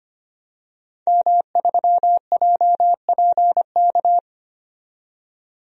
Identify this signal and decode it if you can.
Morse code, heard as a keyed tone, about 25 wpm, 700 Hz. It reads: M3JPK
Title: DXCC-100 Call Signs - 5 Characters - 25wpm